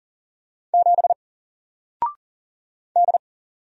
Morse code, heard as a keyed tone, about 40 wpm, 700 Hz. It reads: 7 D
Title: Single Character (Letter:Number:Punctuation:Prosign) 40wpm